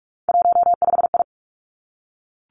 Morse code, heard as a keyed tone, about 46 wpm, 700 Hz. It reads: 15I
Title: Three Challenging Letter-Number Combinations 45wpm